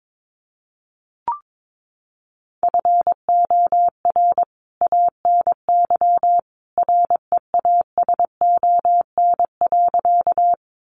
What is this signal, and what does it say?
Morse code, heard as a keyed tone, about 22 wpm, 700 Hz. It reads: FOR ANY REASON.
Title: A Sound Of Thunder 22wpm